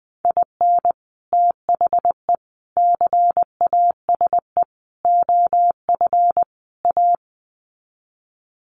Morse code, heard as a keyed tone, about 20 wpm, 700 Hz. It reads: IN THE CASE OF A
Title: Sets of 5 Words Encore 20wpm